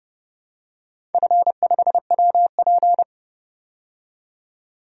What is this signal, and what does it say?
Morse code, heard as a keyed tone, about 32 wpm, 700 Hz. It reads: F5WP
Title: DXCC-100 Call Signs - 4 Characters - 30wpm